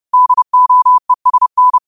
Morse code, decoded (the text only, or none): DOESN